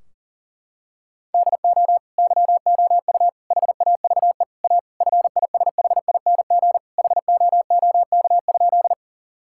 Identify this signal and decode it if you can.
Morse code, heard as a keyed tone, about 40 wpm, 700 Hz. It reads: DO YOU HAVE A FISHING HOOK?
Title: Sentences from Top 1600 Words 40wpm